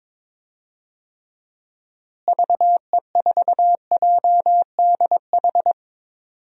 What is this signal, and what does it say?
Morse code, heard as a keyed tone, about 22 wpm, 700 Hz. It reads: VE4JDH
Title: DXCC-100 Call Signs - 6 Characters - Repeated 2x - 22wpm